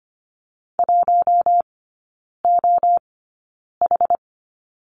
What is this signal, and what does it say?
Morse code, heard as a keyed tone, about 25 wpm, spaced as for 9 wpm, 700 Hz. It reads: 1OH
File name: Lesson 14e - Number 2 - Three Characters 25wpm